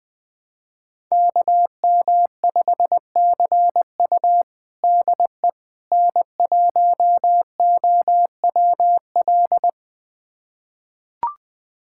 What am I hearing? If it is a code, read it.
Morse code, heard as a keyed tone, about 20 wpm, 700 Hz. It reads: KM5CU DE N1OWL
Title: USA Callsign DE Callsign - 5 Characters - 20wpm